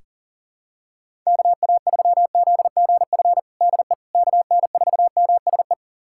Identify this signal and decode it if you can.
Morse code, heard as a keyed tone, about 40 wpm, 700 Hz. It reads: KA3ZGF DE KN4MSE
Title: USA Callsign DE Callsign 40wpm